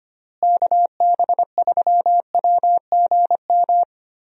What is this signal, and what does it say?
Morse code, heard as a keyed tone, about 25 wpm, 700 Hz. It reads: KB3WGM
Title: USA Call Signs - 6 Characters - Repeated 3x - 25wpm